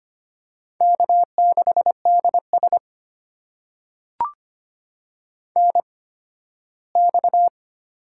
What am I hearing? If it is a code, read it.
Morse code, heard as a keyed tone, about 25 wpm, spaced as for 15 wpm, 700 Hz. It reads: K6DS NX
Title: USA Call Signs - 4 Characters - 8x Character Spacing - 25wpm